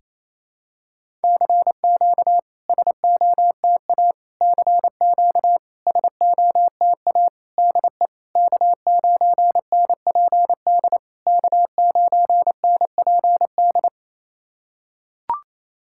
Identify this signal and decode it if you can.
Morse code, heard as a keyed tone, about 28 wpm, 690 Hz. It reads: CQ SOTA CQ SOTA DE K9NPD K9NPD
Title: SOTA QSO Practice 28wpm